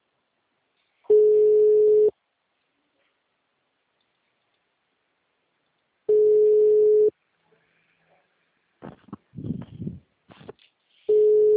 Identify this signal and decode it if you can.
Ringback tone (Europe, 425 Hz).